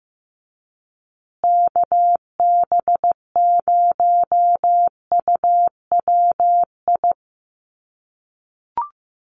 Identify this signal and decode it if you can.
Morse code, heard as a keyed tone, about 15 wpm, 700 Hz. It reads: KB0UWI
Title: USA Call Signs Repeated 2x 15wpm